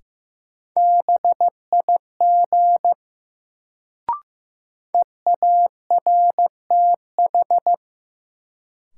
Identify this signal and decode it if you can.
Morse code, heard as a keyed tone, about 16 wpm, 700 Hz. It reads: BIG EARTH
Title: Sentences from Top 200 Words - New Word Drill - 15wpm